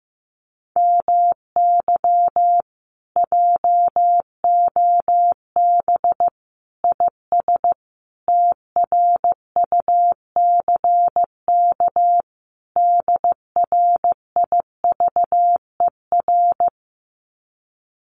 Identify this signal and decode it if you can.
Morse code, heard as a keyed tone, about 15 wpm, 700 Hz. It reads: MY JOB IS TRUCK DRIVER
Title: Common QSO Phrases Encore 15wpm